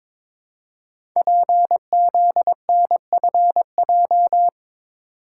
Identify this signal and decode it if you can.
Morse code, heard as a keyed tone, about 23 wpm, 700 Hz. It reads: PZNFJ
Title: Code Groups - 5 Letters - 22wpm